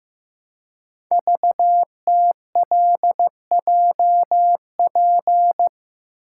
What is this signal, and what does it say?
Morse code, heard as a keyed tone, about 15 wpm, 700 Hz. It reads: VTLJP
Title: Code Groups - 5 Letters - 15wpm